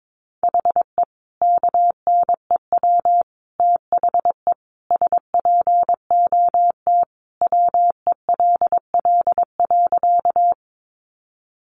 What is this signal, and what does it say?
Morse code, heard as a keyed tone, about 22 wpm, 700 Hz. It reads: HE KNEW THE SPOT WELL.
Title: Sentences from Top 1100 Words 22wpm